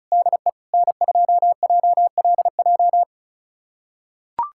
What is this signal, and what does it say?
Morse code, heard as a keyed tone, about 36 wpm, 700 Hz. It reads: DE N2JLJ
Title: USA Callsign DE Callsign - 5 Characters - 35wpm